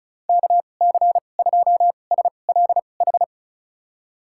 Morse code, heard as a keyed tone, about 36 wpm, spaced as for 25 wpm, 700 Hz. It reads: KC2SLH
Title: USA Call Signs - 6 Characters - 2x Character Spacing - 35wpm